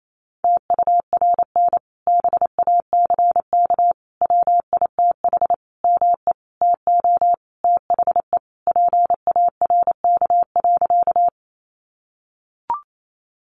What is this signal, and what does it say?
Morse code, heard as a keyed tone, about 28 wpm, 700 Hz. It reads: TURN BACK WITH ME TO THE PARK.
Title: Sentences from Top 1300 Words 28wpm